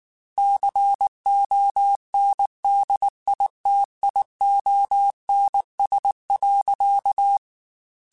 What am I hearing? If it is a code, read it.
Morse code, heard as a keyed tone, about 19 wpm, 800 Hz. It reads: CONDITIONS.
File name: Technical Subjects / Technical Subjects-mp3s / AND Gate @18